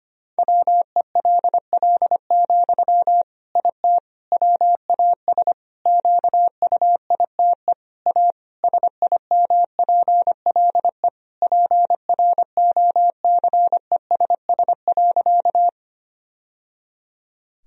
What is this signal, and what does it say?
Morse code, heard as a keyed tone, about 25 wpm, 700 Hz. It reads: WELL, IT WAS QUITE A SIMPLE PROCESS.
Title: Sentences from Top 1000 Words - Review Part 3 - 25wpm